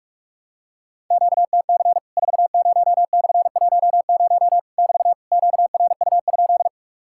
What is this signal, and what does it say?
Morse code, heard as a keyed tone, about 45 wpm, 700 Hz. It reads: QTC 40/10 <BT> QRU?
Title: Worked All Europe DX Contest - QTC Exchange - 45wpm